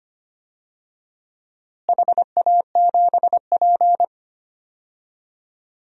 Morse code, heard as a keyed tone, about 25 wpm, 700 Hz. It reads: HA7P
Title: DXCC-100 Call Signs - 4 Characters - Repeated 3x - 25wpm